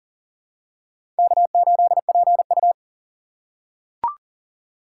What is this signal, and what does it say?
Morse code, heard as a keyed tone, about 38 wpm, 700 Hz. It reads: K8PU
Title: USA Call Signs - 4x Character Spacing - 40wpm